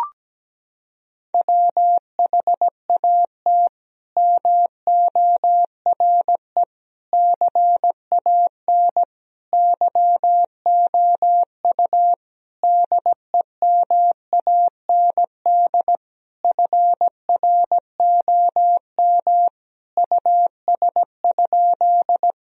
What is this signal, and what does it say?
Morse code, heard as a keyed tone, about 17 wpm, 700 Hz. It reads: WHAT MORE CAN YOU DEMAND FROM US?
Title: Sentences from Top 1000 Words - Review Part 4 - 17wpm